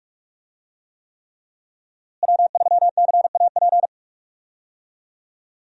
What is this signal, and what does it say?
Morse code, heard as a keyed tone, about 46 wpm, 700 Hz. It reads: W3CAP
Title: USA Call Signs - 5 Characters - Repeated 3x - 45wpm